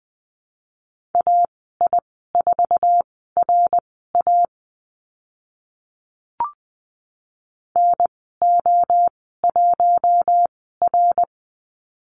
Morse code, heard as a keyed tone, about 20 wpm, spaced as for 15 wpm, 700 Hz. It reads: AI4RA NO1R
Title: 062 Lesson 11g - Letter h - Call Signs 20wpm